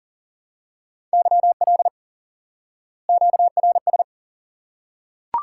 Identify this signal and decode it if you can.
Morse code, heard as a keyed tone, about 40 wpm, 700 Hz. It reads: YL QRS
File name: IQE Course 3 - Lesson 89 - YAGI - Two Elements - 40wpm